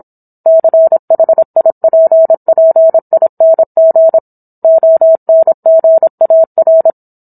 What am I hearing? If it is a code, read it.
Morse code, heard as a keyed tone, about 26 wpm, 650 Hz. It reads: CHIPPING ONGAR